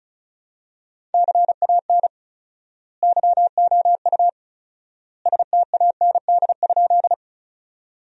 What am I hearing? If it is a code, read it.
Morse code, heard as a keyed tone, about 36 wpm, 700 Hz. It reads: CAN YOU STAND?
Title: 4x Word-Spacing - Sentences from Top 300 Words - 35wpm